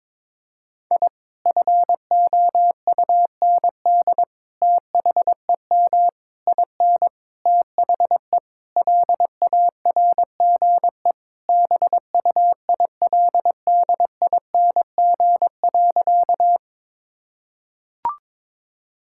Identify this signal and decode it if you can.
Morse code, heard as a keyed tone, about 22 wpm, 700 Hz. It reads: I FOUND THEM IN THE LARGE BUILDING.
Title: Sentences from Top 1000 Words - Review Part 5 - 22wpm